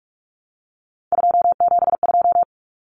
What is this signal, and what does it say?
Morse code, heard as a keyed tone, about 44 wpm, 700 Hz. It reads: 272
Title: Numbers - 3-Digit 45wpm